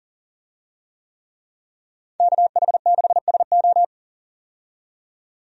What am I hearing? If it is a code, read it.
Morse code, heard as a keyed tone, about 40 wpm, 700 Hz. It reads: KH6SO
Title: DXCC-100 Call Signs - 5 Characters - 40wpm